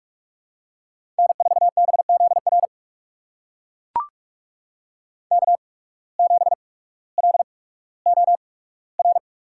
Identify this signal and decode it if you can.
Morse code, heard as a keyed tone, about 42 wpm, spaced as for 19 wpm, 700 Hz. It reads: N4BZR K7LOR
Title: USA Call Signs - 5 Characters - 8x Character Spacing - 45wpm